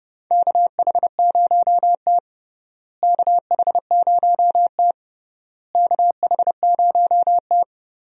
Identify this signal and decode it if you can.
Morse code, heard as a keyed tone, about 30 wpm, 700 Hz. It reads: KH0T KH0T KH0T
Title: USA Call Signs - 4 Characters - Repeated 3x - 30wpm